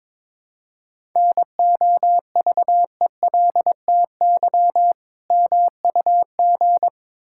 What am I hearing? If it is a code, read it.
Morse code, heard as a keyed tone, about 22 wpm, 690 Hz. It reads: NOVELTY MUG
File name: Difficult Words with Hints - 22wpm